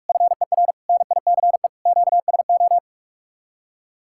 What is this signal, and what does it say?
Morse code, heard as a keyed tone, about 44 wpm, 700 Hz. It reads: FER NICE QSO